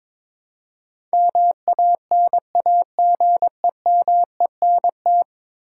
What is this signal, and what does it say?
Morse code, heard as a keyed tone, about 22 wpm, 700 Hz. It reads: MANAGEMENT
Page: morse-code-ninja-practice-22wpm